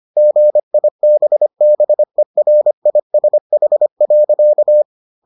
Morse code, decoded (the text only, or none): GIBBERISH.